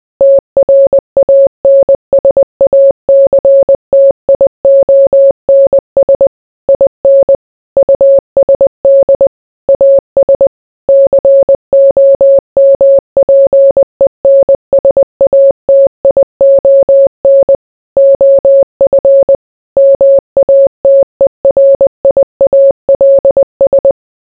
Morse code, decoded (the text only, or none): TRANSACTIONS IN USD AS COMPENSATION OF MATERIALS